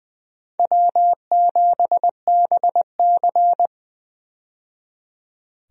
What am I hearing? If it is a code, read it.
Morse code, heard as a keyed tone, about 20 wpm, 700 Hz. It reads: W7BC
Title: USA Call Signs - 4 Characters - 20wpm